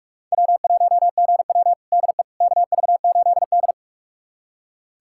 Morse code, heard as a keyed tone, about 46 wpm, 700 Hz. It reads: W1GW DE KV8D
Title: USA Callsign DE Callsign - 4 Characters - 45wpm